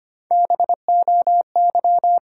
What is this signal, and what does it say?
Morse code, heard as a keyed tone, about 25 wpm, 700 Hz. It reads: BOY
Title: Three Letter Words - 25wpm